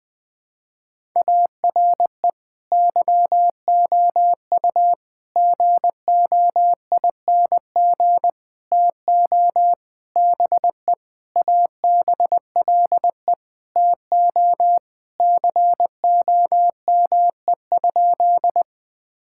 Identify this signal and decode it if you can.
Morse code, heard as a keyed tone, about 20 wpm, 700 Hz. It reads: ARE YOU GOING TO BE ABLE TO COME?
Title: Sentences with 5-Word Phrases - 20wpm